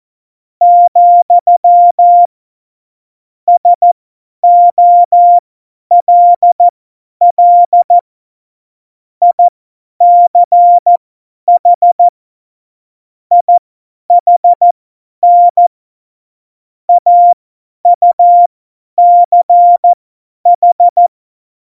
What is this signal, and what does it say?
Morse code, heard as a keyed tone, about 14 wpm, spaced as for 10 wpm, 700 Hz. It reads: , SOLL ICH IHN AUCH
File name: Morsekurs / Witze / Witz 12. Februar 2019, 50 F70